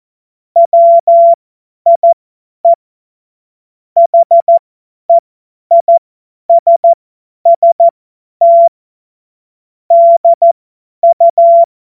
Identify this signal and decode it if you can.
Morse code, heard as a keyed tone, about 14 wpm, spaced as for 10 wpm, 680 Hz. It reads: WIE HEISST DU